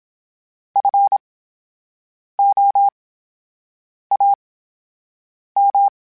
Morse code, heard as a keyed tone, about 27 wpm, spaced as for 7 wpm, 800 Hz. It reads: FOAM
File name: Special INGs / Special INGs-mp3s / VerySlowToFaster Ing @25